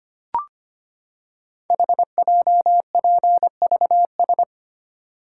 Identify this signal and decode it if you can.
Morse code, heard as a keyed tone, about 25 wpm, 700 Hz. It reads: HJPVS
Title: Code Groups - 5 Letters - 25wpm